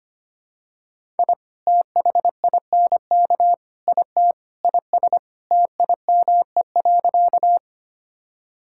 Morse code, heard as a keyed tone, about 25 wpm, 700 Hz. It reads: I THINK IT IS TIME.
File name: practice / 25wpm / Sentences with 4-Word Phrases - 25wpm